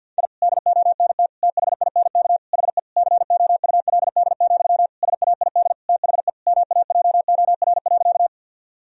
Morse code, heard as a keyed tone, about 52 wpm, 700 Hz. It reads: I DONT THINK HE COULD, SAID THE MAJOR.
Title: Sentences from Top 1000 Words - Review Part 4 - 50wpm